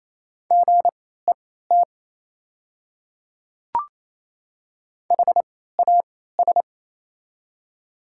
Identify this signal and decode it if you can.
Morse code, heard as a keyed tone, about 28 wpm, spaced as for 16 wpm, 700 Hz. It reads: GET HAS
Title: 3x Character Spacing - Top 200 Words - 28wpm